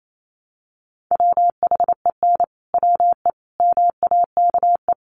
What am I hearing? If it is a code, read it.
Morse code, heard as a keyed tone, about 28 wpm, 700 Hz. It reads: WHEN WE MAKE
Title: Sentences from Top 1100 Words 28wpm